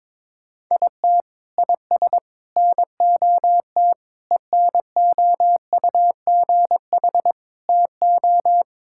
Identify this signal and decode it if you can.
Morse code, heard as a keyed tone, about 22 wpm, 700 Hz. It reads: IT IS NOT ENOUGH TO
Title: Sentences from Top 1000 Words - Review Part 6 - 22wpm